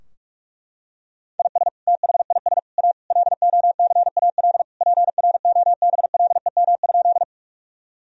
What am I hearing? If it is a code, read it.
Morse code, heard as a keyed tone, about 41 wpm, 700 Hz. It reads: IS THIS A LOCAL PROBLEM?
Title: Sentences from Top 1000 Words - Review Part 3 - 45wpm